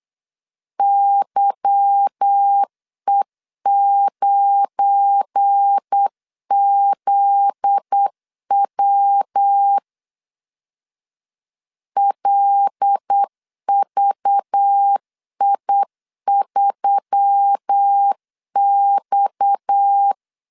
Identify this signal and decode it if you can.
Morse code, heard as a keyed tone, about 8 wpm, 800 Hz. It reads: YE9ZW LVI3X
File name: LEZIONI / 8 YJ45 / 8 Y 8wpm 3245